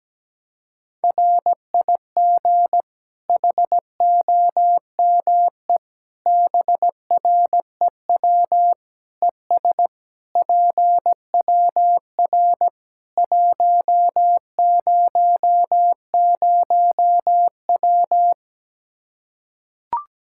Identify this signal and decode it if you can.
Morse code, heard as a keyed tone, about 17 wpm, 700 Hz. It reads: RIG HOME BREW ES PWR 100W
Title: Practice QSO - Exchange 2 - Info - 17wpm